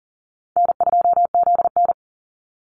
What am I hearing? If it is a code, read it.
Morse code, heard as a keyed tone, about 40 wpm, 700 Hz. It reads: N2ZN
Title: Easy Call Signs Repeated 3x - 40wpm